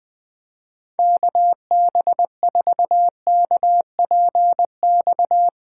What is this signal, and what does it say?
Morse code, heard as a keyed tone, about 20 wpm, 700 Hz. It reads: KB4KPX
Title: Speed Racing - USA 2x3 Call Signs 17wpm